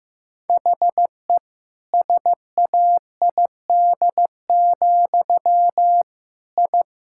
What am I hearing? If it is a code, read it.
Morse code, heard as a keyed tone, about 15 wpm, 700 Hz. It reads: HE SAID, I
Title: Sentences from Top 1000 Words - Review Part 6 - 15wpm